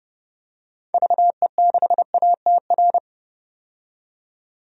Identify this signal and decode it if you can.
Morse code, heard as a keyed tone, about 31 wpm, 700 Hz. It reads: VE6ATR
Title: DXCC-100 Call Signs - 6 Characters - 30wpm